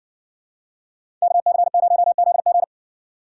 Morse code, heard as a keyed tone, about 55 wpm, 700 Hz. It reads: KC0ZG